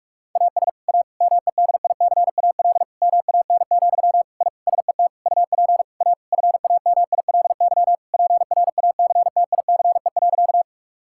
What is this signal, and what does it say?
Morse code, heard as a keyed tone, about 46 wpm, 700 Hz. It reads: AS A MEDICAL MAN, I SET UP A FAMILY PRACTICE.
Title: Sentences from Top 1000 Words - Review Part 5 - 45wpm